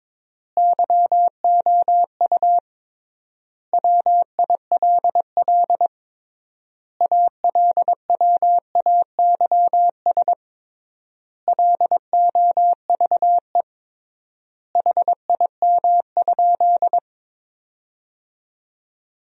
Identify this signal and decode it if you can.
Morse code, heard as a keyed tone, about 22 wpm, 700 Hz. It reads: YOU WILL ALWAYS LOVE HIM?
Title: 3x Word-Spacing - Sentences from Top 400 Words - 22wpm